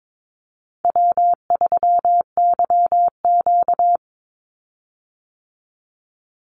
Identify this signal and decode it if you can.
Morse code, heard as a keyed tone, about 22 wpm, 700 Hz. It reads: W3YQ
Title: Easy Call Signs Repeated 2x - 22wpm